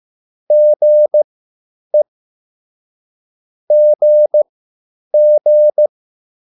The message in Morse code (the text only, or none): GE GG